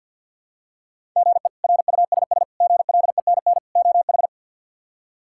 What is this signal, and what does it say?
Morse code, heard as a keyed tone, about 47 wpm, 700 Hz. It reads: GE RUSS GLENN OH
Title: K1USN SST QSO Practice - 50wpm